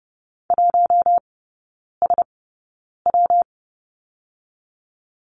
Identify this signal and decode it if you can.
Morse code, heard as a keyed tone, about 30 wpm, 700 Hz. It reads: 1 S W
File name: Lesson 19e - Letter w - Three Characters 30wpm